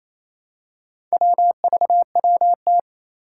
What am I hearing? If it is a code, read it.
Morse code, heard as a keyed tone, about 28 wpm, 700 Hz. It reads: WVWT
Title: Four Challenging Letter-Number Combinations 28wpm